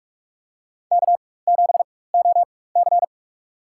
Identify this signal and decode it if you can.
Morse code, heard as a keyed tone, about 46 wpm, spaced as for 21 wpm, 700 Hz. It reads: K7OC
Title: USA Call Signs - 4 Characters - 4x Character Spacing - 45wpm